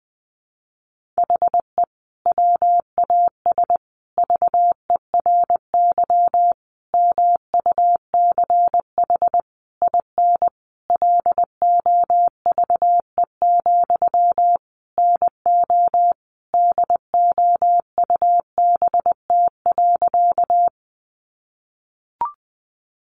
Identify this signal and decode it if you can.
Morse code, heard as a keyed tone, about 20 wpm, 700 Hz. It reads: HE WAS VERY MUCH IN LOVE, NO DOUBT.
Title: Sentences from Top 1500 Words 20wpm